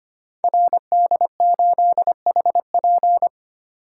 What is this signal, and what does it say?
Morse code, heard as a keyed tone, about 25 wpm, 700 Hz. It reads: RD8HP
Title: Code Groups - 5 Letter:Numbers - 25wpm